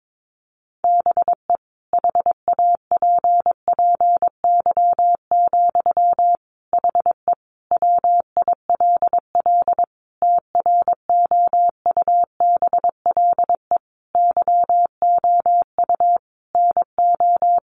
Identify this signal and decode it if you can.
Morse code, heard as a keyed tone, about 22 wpm, 700 Hz. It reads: BE HAPPY, HE WILL TROUBLE YOU NO
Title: Sentences from Top 1200 Words 22wpm